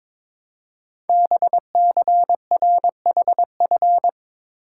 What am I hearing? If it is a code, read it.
Morse code, heard as a keyed tone, about 22 wpm, 700 Hz. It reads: BCRHF
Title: Code Groups - 5 Letters - 22wpm